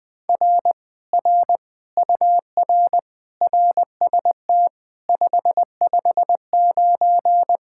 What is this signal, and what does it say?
Morse code, heard as a keyed tone, about 20 wpm, 700 Hz. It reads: R R UR RST 559
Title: Speed-Racing Common QSO Phrases - 17wpm